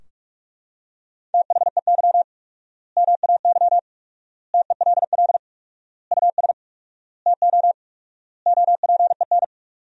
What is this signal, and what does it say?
Morse code, heard as a keyed tone, about 41 wpm, 700 Hz. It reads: THEY MAY TELL US TO OPEN
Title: 4x Word-Spacing - Sentences from Top 300 Words - 45wpm